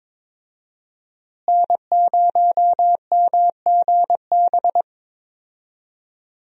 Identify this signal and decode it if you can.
Morse code, heard as a keyed tone, about 22 wpm, 700 Hz. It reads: N0MGB
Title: USA Call Signs - 5 Characters - Repeated 3x - 22wpm